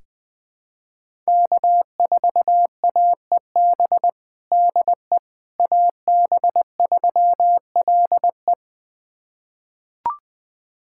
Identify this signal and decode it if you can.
Morse code, heard as a keyed tone, about 19 wpm, 700 Hz. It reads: K4AEB DE AB3LE
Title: USA Callsign DE Callsign - 5 Characters - 20wpm